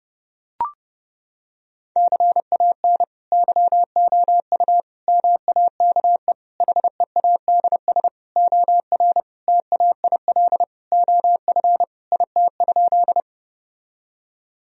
Morse code, heard as a keyed tone, about 32 wpm, 700 Hz. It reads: CAN YOU MAKE HEADS OR TAIL OF IT?
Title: Sentences from Top 1000 Words - Review Part 3 - 30wpm